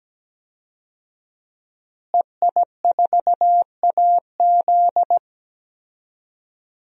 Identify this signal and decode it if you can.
Morse code, heard as a keyed tone, about 17 wpm, 700 Hz. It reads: EI4AZ
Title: DXCC-100 Call Signs - 5 Characters - Repeated 2x - 17wpm